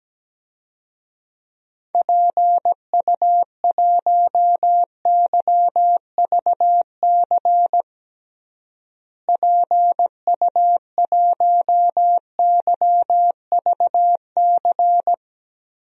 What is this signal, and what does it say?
Morse code, heard as a keyed tone, about 17 wpm, 700 Hz. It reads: PU1YVC PU1YVC
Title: DXCC-100 Call Signs - 6 Characters - Repeated 3x - 17wpm